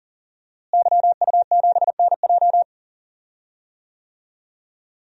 Morse code, heard as a keyed tone, about 40 wpm, 700 Hz. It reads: YU7NJ
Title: DXCC-100 Call Signs - 5 Characters - Repeated 2x - 40wpm